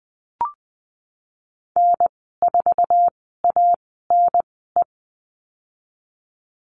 Morse code, heard as a keyed tone, about 20 wpm, spaced as for 15 wpm, 700 Hz. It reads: N4ANE
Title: 055 Lesson 10g - Letter r - Call Signs 20wpm